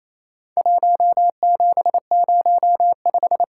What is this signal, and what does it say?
Morse code, heard as a keyed tone, about 28 wpm, 700 Hz. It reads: 1705
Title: Worked All Europe DX Contest - QTC Exchange - 28wpm